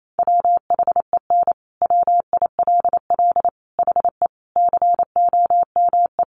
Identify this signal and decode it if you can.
Morse code, heard as a keyed tone, about 28 wpm, 700 Hz. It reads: WHEN WILL HE COME
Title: Sentences from Top 200 Words 28wpm